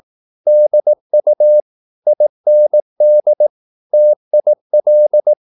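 Morse code, heard as a keyed tone, about 18 wpm, 600 Hz. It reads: DU IND TIL
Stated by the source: Slow (QRS) Morse practice at 18 wpm, CW med Gnister.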